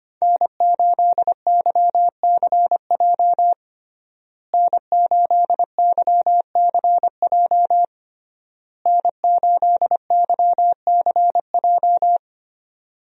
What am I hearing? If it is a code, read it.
Morse code, heard as a keyed tone, about 25 wpm, 700 Hz. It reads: N8YCJ N8YCJ N8YCJ
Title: USA Call Signs - 5 Characters - Repeated 3x - 25wpm